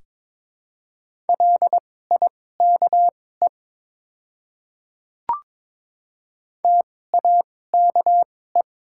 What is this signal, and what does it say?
Morse code, heard as a keyed tone, about 22 wpm, spaced as for 16 wpm, 700 Hz. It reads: LIKE TAKE
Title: 2x Character Spacing - Top 200 Words - 22wpm